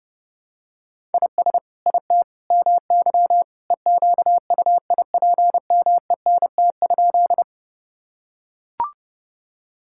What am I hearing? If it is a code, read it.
Morse code, heard as a keyed tone, about 30 wpm, 700 Hz. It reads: IS IT MY EQUIPMENT?
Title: Sentences from Top 1000 Words - Review Part 5 - 30wpm